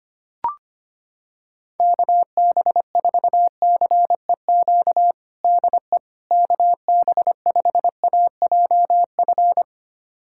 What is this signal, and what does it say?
Morse code, heard as a keyed tone, about 25 wpm, 700 Hz. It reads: KB4CEQ DE KB5AJF
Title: USA Callsign DE Callsign 25wpm